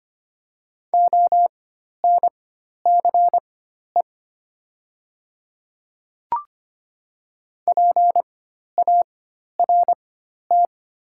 Morse code, heard as a keyed tone, about 25 wpm, spaced as for 12 wpm, 700 Hz. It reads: ONCE PART
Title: 4x Character Spacing - Top 300 Words - 25wpm